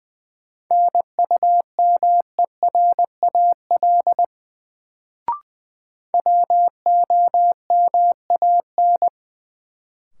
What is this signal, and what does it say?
Morse code, heard as a keyed tone, about 20 wpm, 700 Hz. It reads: NUMERAL WOMAN
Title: Sentences from Top 1000 Words - Review Part 2 - New Word Drill - 20wpm